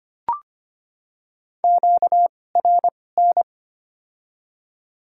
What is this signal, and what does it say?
Morse code, heard as a keyed tone, about 25 wpm, spaced as for 18 wpm, 700 Hz. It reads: QRN
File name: Top 90 QSO Elements - 2x Character Spacing - 25wpm